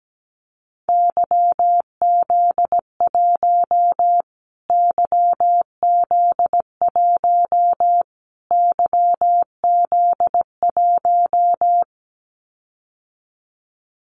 Morse code, heard as a keyed tone, about 17 wpm, 700 Hz. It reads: YZ1 YZ1 YZ1
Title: Callsign Prefixes for Top 25 DXCC - Repeated 3x - 17wpm